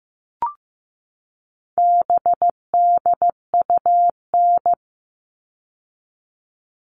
Morse code, heard as a keyed tone, about 15 wpm, 700 Hz. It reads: BDUN
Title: All Cut Numbers - 4-Digit 15wpm